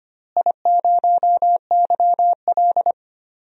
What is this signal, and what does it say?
Morse code, heard as a keyed tone, about 25 wpm, 700 Hz. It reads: I0YL
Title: DXCC-100 Call Signs - 4 Characters - Repeated 3x - 25wpm